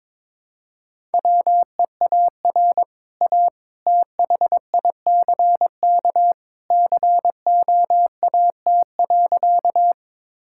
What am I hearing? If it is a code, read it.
Morse code, heard as a keyed tone, about 22 wpm, 700 Hz. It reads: WEAR A THICK COAT.
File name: practice / 22wpm / Easy Sentences with Hints - 22wpm